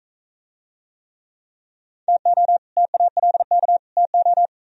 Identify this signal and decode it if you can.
Morse code, heard as a keyed tone, about 43 wpm, 700 Hz. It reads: TO TALK TO